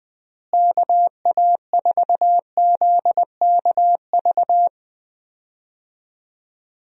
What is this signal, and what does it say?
Morse code, heard as a keyed tone, about 20 wpm, 700 Hz. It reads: KA4ZKV
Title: USA Call Signs - 6 Characters - Repeated 3x - 20wpm